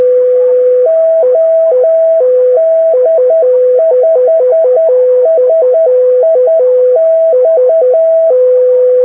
Morse code, heard as a keyed tone, about 10 wpm, 650 Hz, with noise. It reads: OD5SIX
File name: vhf / soundclips6m / od5six 6m